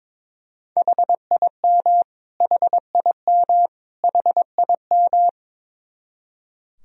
Morse code, heard as a keyed tone, about 22 wpm, 700 Hz. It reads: HIM HIM HIM
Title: Top 100 Words Repeated 22wpm